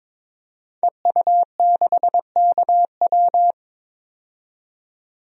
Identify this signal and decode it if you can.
Morse code, heard as a keyed tone, about 21 wpm, 700 Hz. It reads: EU6KW
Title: DXCC-100 Call Signs - 5 Characters - 22wpm